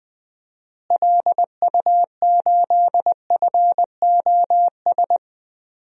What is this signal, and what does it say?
Morse code, heard as a keyed tone, about 20 wpm, 700 Hz. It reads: LU8FOS
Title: DXCC-100 Call Signs - 6 Characters - Repeated 3x - 20wpm